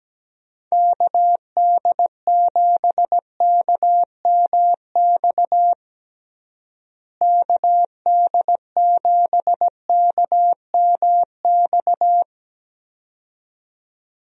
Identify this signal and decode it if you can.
Morse code, heard as a keyed tone, about 17 wpm, 700 Hz. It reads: KD7KMX KD7KMX
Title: USA Call Signs - 6 Characters - Repeated 2x - 17wpm